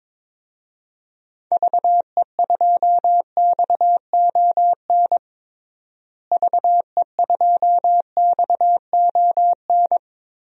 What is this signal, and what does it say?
Morse code, heard as a keyed tone, about 22 wpm, 700 Hz. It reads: VE2XON VE2XON
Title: DXCC-100 Call Signs - 6 Characters - Repeated 2x - 22wpm